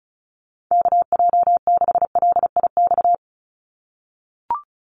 Morse code, heard as a keyed tone, about 34 wpm, 700 Hz. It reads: KJ6LIX
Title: USA Call Signs 35wpm